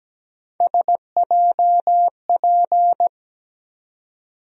Morse code, heard as a keyed tone, about 17 wpm, 700 Hz. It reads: SJP
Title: USA Call Sign Postfixes - 3 Letters 17wpm